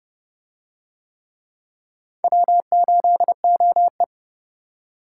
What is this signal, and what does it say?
Morse code, heard as a keyed tone, about 31 wpm, 700 Hz. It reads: W8OE